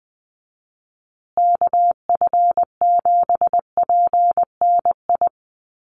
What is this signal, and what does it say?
Morse code, heard as a keyed tone, about 20 wpm, 700 Hz. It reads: KF7PNI
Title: USA Call Signs Repeated 3x 20wpm